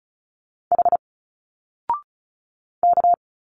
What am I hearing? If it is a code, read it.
Morse code, heard as a keyed tone, about 36 wpm, 700 Hz. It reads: H K
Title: Single Letters 35wpm